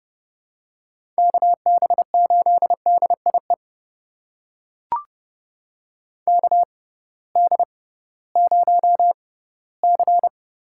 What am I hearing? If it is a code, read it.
Morse code, heard as a keyed tone, about 30 wpm, spaced as for 17 wpm, 700 Hz. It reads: KB8DIE KD0C
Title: USA Call Signs - 6 Characters - 6x Character Spacing - 30wpm